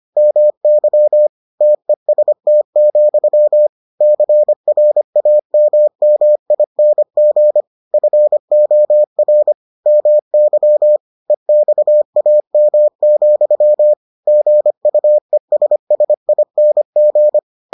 Morse code, heard as a keyed tone, about 25 wpm, 600 Hz. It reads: MY TEST, CRAMMING FOR MY EXAM, GUESSING